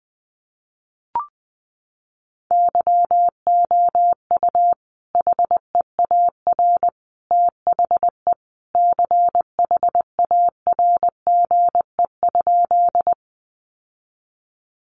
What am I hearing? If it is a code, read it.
Morse code, heard as a keyed tone, about 20 wpm, 700 Hz. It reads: YOU HEAR THE CHARGE?
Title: Sentences from Top 900 Words 20wpm